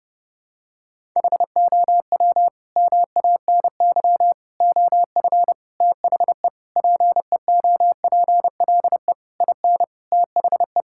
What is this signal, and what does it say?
Morse code, heard as a keyed tone, about 30 wpm, 700 Hz. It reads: HOW MANY OF THE PEOPLE IN THE
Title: Sentences from Top 1000 Words - Review Part 6 - 30wpm